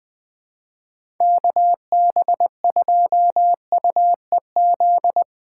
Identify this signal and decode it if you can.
Morse code, heard as a keyed tone, about 20 wpm, 700 Hz. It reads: KB2UEZ
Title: USA Call Signs - 6 Characters - 4x Character Spacing - 20wpm